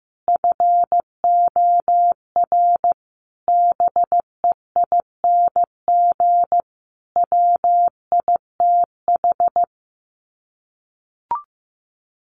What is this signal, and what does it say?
Morse code, heard as a keyed tone, about 15 wpm, 700 Hz. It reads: FOR BEING WITH
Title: Sets of 5 Words 15wpm